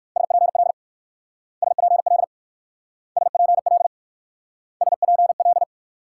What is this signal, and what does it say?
Morse code, heard as a keyed tone, about 54 wpm, 700 Hz. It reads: SPL SPL SPL SPL
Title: Speed-Racing Multi-letter Phonemes - 45wpm